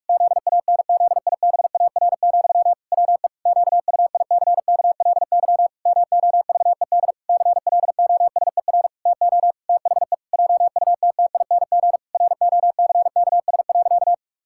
Morse code, heard as a keyed tone, about 46 wpm, 700 Hz. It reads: ZANZIBAR, WE QUICKLY MOVED CLOSER TO THE JUTTING ROCKS.
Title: Pangram Challenge v2 - 45wpm